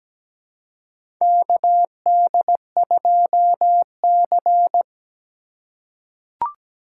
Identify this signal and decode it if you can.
Morse code, heard as a keyed tone, about 17 wpm, 700 Hz. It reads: KD2C
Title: USA Call Signs - 4x Character Spacing - 17wpm